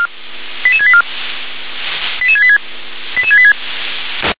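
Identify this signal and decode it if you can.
Five-tone selective call (ZVEI), heard as quick runs of five tones, tones of about 70 ms. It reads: 88684 88686
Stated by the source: ZVEI five-tone selective call 88738 received on SDR.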